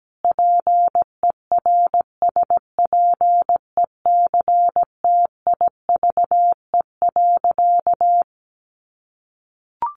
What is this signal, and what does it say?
Morse code, heard as a keyed tone, about 17 wpm, 700 Hz. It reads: PERSPECTIVE.
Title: Sentences from Top 1200 Words 17wpm